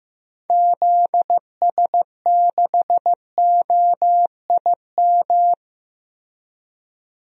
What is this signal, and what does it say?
Morse code, heard as a keyed tone, about 16 wpm, 700 Hz. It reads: ZS6OIM
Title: DXCC-100 Call Signs - 6 Characters - 15wpm